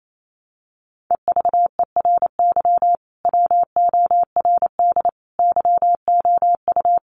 Morse code, heard as a keyed tone, about 28 wpm, 700 Hz. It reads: EVERY WORD YOU
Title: Sentences from Top 400 Words 28wpm